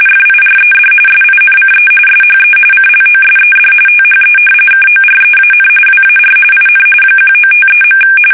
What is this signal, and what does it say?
RTTY, about 45.45 baud. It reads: FIRED SINCE SHE CLOSED THE KOREAN WAR IN 1953.
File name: fsk850 / battleshipnj